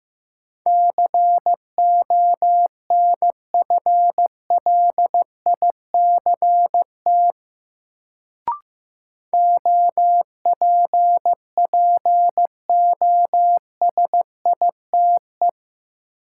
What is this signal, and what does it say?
Morse code, heard as a keyed tone, about 15 wpm, 700 Hz. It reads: CONFLICT OPPOSITE
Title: Sentences from Top 1300 Words - New Word Drill - 15wpm